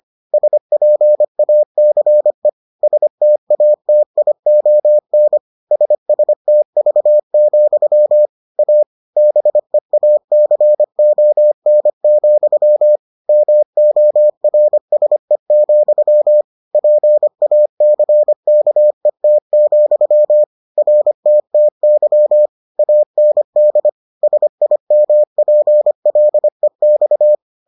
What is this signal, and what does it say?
Morse code, heard as a keyed tone, about 25 wpm, 600 Hz. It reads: SPACE STATION SSTV, A BEACON, MORSE, PACKET, RTTY AND SIMPLEX